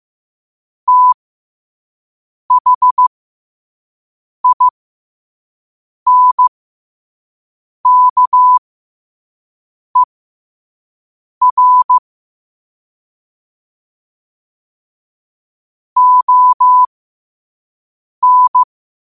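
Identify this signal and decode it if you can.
Morse code, heard as a keyed tone, about 15 wpm, spaced as for 5 wpm, 1000 Hz. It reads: THINKER ON
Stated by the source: Morse code sent at 5 wpm.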